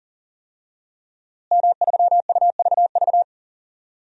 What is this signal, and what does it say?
Morse code, heard as a keyed tone, about 40 wpm, 700 Hz. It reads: M3UVV
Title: DXCC-100 Call Signs - 5 Characters - Repeated 3x - 40wpm